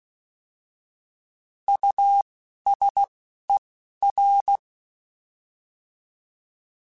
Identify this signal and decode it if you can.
Morse code, heard as a keyed tone, about 16 wpm, spaced as for 11 wpm, 800 Hz. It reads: USER
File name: Alphabetical / Alphabetical mp3s / Us @10